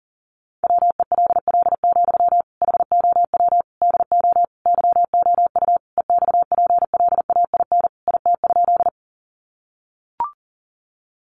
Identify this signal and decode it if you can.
Morse code, heard as a keyed tone, about 40 wpm, 700 Hz. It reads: WELL, HOW DO YOU EXPLAIN IT?
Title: Sentences from Top 1100 Words 40wpm